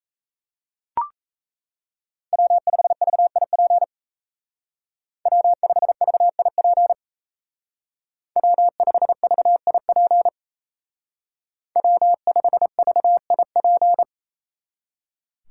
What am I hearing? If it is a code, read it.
Morse code, heard as a keyed tone, about 36 wpm, 700 Hz. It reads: W5VIP W5VIP W5VIP W5VIP